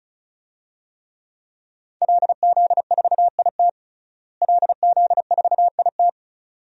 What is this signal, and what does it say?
Morse code, heard as a keyed tone, about 36 wpm, 700 Hz. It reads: LZ4IT LZ4IT
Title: DXCC-100 Call Signs - 5 Characters - Repeated 2x - 35wpm